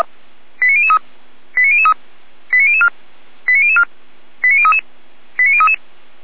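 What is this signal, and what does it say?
Five-tone selective call (ZVEI), heard as quick runs of five tones, tones of about 70 ms. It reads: 89003 89003 89004 89004 89033 89033
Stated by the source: Received on an SDR; ZVEI five-tone alert from the server of Feuerwehr Orlamünde (f85360).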